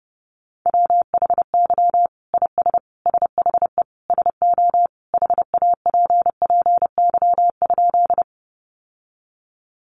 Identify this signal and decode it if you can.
Morse code, heard as a keyed tone, about 30 wpm, 700 Hz. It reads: WHY IS SHE SO HAPPY?
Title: Sentences from Top 1000 Words 30wpm